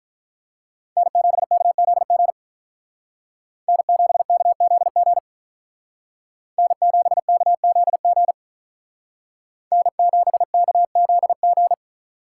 Morse code, heard as a keyed tone, about 43 wpm, 700 Hz. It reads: N7KZG N7KZG N7KZG N7KZG
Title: Speed Racing - USA Call Signs 35wpm